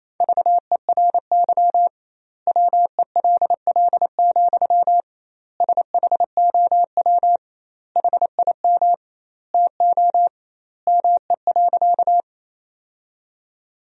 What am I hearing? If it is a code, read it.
Morse code, heard as a keyed tone, about 27 wpm, 700 Hz. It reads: VERY WELL, SHOW HIM TO ME.
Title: 2x Word-Spacing - Sentences from Top 200 Words - 28wpm